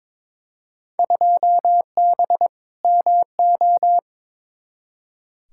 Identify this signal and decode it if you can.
Morse code, heard as a keyed tone, about 22 wpm, 700 Hz. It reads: 2B MO
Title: Field Day Exchanges - 22wpm